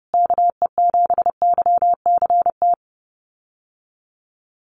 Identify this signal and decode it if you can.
Morse code, heard as a keyed tone, about 28 wpm, 700 Hz. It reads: KE7YCT
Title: USA Call Signs Repeated 2x 30wpm